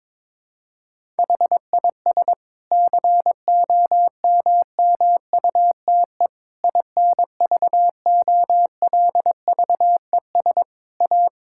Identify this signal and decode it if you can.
Morse code, heard as a keyed tone, about 22 wpm, 700 Hz. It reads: HIS COMMUTE INVOLVES A
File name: Difficult Sentences with Hints - 22wpm